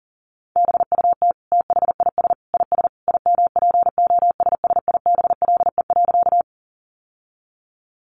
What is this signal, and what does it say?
Morse code, heard as a keyed tone, about 40 wpm, 700 Hz. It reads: BUT THIS IS IMPOSSIBLE.
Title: Sentences from Top 1100 Words 40wpm